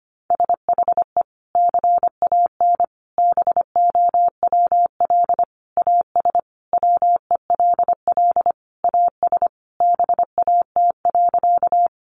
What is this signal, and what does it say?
Morse code, heard as a keyed tone, about 25 wpm, 700 Hz. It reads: SHE CAN BOWL AS WELL AS BAT.
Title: Sentences from Top 1500 Words 25wpm